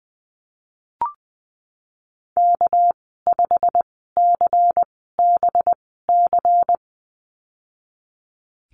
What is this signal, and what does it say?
Morse code, heard as a keyed tone, about 20 wpm, spaced as for 15 wpm, 700 Hz. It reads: K5CBC
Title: 219 Lesson 35g - Number 0 - Call Signs 20wpm